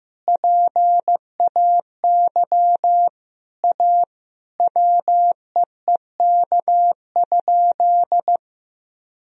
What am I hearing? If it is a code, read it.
Morse code, heard as a keyed tone, about 16 wpm, 700 Hz. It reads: PAY A WEEK?
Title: Sentences from Top 1000 Words - Review Part 2 - 15wpm